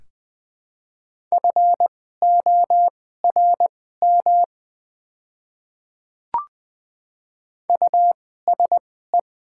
Morse code, heard as a keyed tone, about 20 wpm, spaced as for 15 wpm, 700 Hz. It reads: FORM USE
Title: 2x Character Spacing - Top 200 Words - 20wpm